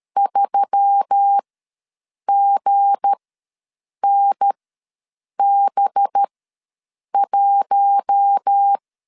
Morse code, heard as a keyed tone, about 13 wpm, spaced as for 7 wpm, 800 Hz. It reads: 3GNB1
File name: LEZIONI / 9 2 ws 12wpm 3406